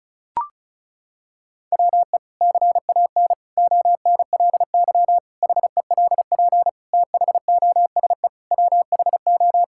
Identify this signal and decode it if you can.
Morse code, heard as a keyed tone, about 36 wpm, 700 Hz. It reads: WE CAN ONLY HELP THOSE WHO
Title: Sentences from Top 1000 Words - Review Part 5 - 35wpm